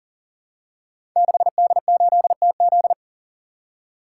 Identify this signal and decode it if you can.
Morse code, heard as a keyed tone, about 43 wpm, 700 Hz. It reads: 6D8TZ
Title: Code Groups - 5 Letter:Numbers - 40wpm